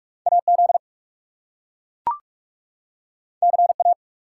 Morse code, heard as a keyed tone, about 43 wpm, 700 Hz. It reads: AZ CA
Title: US States and Canadian Provinces 45wpm